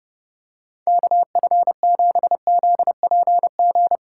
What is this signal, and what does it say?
Morse code, heard as a keyed tone, about 29 wpm, 700 Hz. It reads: KF7ZPG